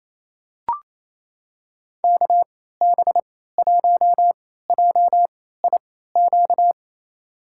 Morse code, heard as a keyed tone, about 28 wpm, spaced as for 16 wpm, 700 Hz. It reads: KB1JIQ
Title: USA Call Signs - 6 Characters - 3x Character Spacing - 28wpm